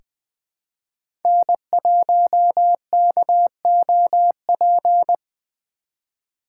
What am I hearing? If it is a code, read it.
Morse code, heard as a keyed tone, about 20 wpm, 700 Hz. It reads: N1KOP
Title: USA Call Signs - 5 Characters - 6x Character Spacing - 20wpm